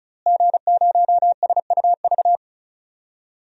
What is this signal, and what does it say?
Morse code, heard as a keyed tone, about 35 wpm, 700 Hz. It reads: G0SUV
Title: DXCC-100 Call Signs - 5 Characters - Repeated 2x - 35wpm